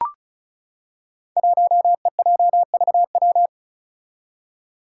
Morse code, heard as a keyed tone, about 35 wpm, 700 Hz. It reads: 1EJVW
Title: Five Challenging Letter-Number Combinations 35wpm